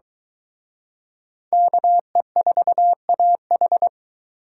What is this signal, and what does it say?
Morse code, heard as a keyed tone, about 23 wpm, 700 Hz. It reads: KE4AH